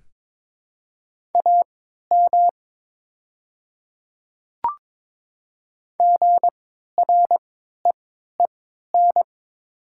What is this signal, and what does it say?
Morse code, heard as a keyed tone, about 22 wpm, spaced as for 12 wpm, 690 Hz. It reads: AM GREEN
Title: 3x Character Spacing - Top 500 Words - 22wpm